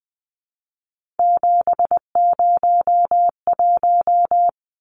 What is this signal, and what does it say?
Morse code, heard as a keyed tone, about 20 wpm, 700 Hz. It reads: 701
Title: Numbers - 3-Digit 20wpm